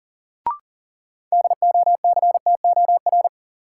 Morse code, heard as a keyed tone, about 40 wpm, 700 Hz. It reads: DOCTOR
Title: Sentences from Top 1500 Words - New Word Drill Alt - 40wpm